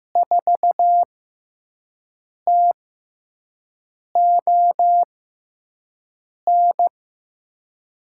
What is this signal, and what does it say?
Morse code, heard as a keyed tone, about 15 wpm, spaced as for 5 wpm, 700 Hz. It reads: 4TON
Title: USA Call Signs - 6 Characters - 6x Character Spacing - 15wpm